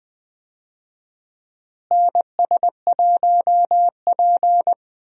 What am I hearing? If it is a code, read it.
Morse code, heard as a keyed tone, about 20 wpm, 700 Hz. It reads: NS1P
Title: USA Call Signs - 4 Characters - Repeated 3x - 20wpm